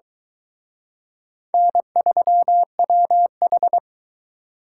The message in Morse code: N3WH

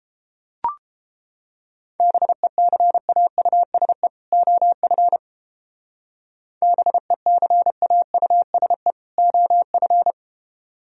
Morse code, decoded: BECAUSE OF BECAUSE OF

